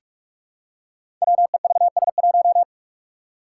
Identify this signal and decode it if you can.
Morse code, heard as a keyed tone, about 44 wpm, 700 Hz. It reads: WEVS1
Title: Five Challenging Letter-Number Combinations 45wpm